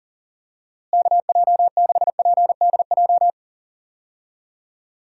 Morse code, heard as a keyed tone, about 40 wpm, 700 Hz. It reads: KJ6PDJ
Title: USA Call Signs - 6 Characters - Repeated 2x - 40wpm